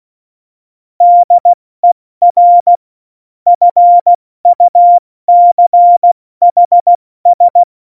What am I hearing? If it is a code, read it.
Morse code, heard as a keyed tone, about 16 wpm, 700 Hz. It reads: DER FUCHS
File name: Morsekurs / Witze / Witz 22. August 17, Tempo 70 F80